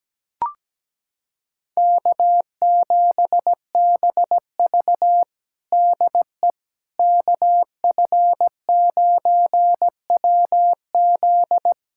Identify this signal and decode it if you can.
Morse code, heard as a keyed tone, about 17 wpm, 700 Hz. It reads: K7BV DE KF9WZ
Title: K1USN SST QSO Practice 17wpm